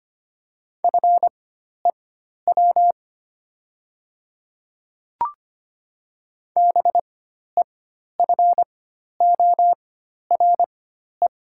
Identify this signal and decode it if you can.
Morse code, heard as a keyed tone, about 25 wpm, spaced as for 12 wpm, 700 Hz. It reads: FEW BEFORE
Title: 4x Character Spacing - Top 300 Words - 25wpm